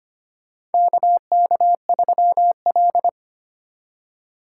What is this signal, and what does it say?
Morse code, heard as a keyed tone, about 25 wpm, 700 Hz. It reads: KK3L
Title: USA Call Signs - 4 Characters - 2x Character Spacing - 25wpm